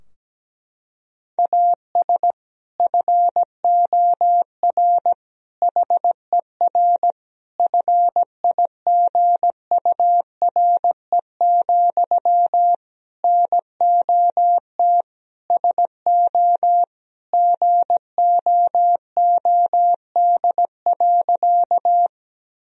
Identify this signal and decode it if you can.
Morse code, heard as a keyed tone, about 17 wpm, 700 Hz. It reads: AS FOR HER FIGURE, NOT SO GOOD.
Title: Sentences from Top 1000 Words - Review Part 2 - 17wpm